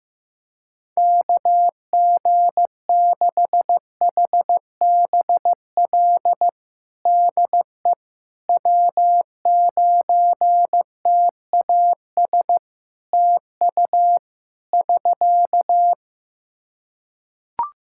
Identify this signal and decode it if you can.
Morse code, heard as a keyed tone, about 15 wpm, 700 Hz. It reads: KG6HBL DE W9TAS TU <SK>
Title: Practice QSO - Exchange 4 - Ending - 15wpm